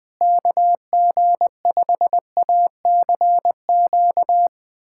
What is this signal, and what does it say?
Morse code, heard as a keyed tone, about 20 wpm, 700 Hz. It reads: KG5ACQ
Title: USA Callsign DE Callsign - 6 Characters - 20wpm